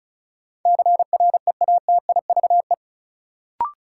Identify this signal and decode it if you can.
Morse code, heard as a keyed tone, about 36 wpm, 700 Hz. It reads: CREATIVE
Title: Sentences from Top 1000 Words - Review Part 5 - New Word Drill - 35wpm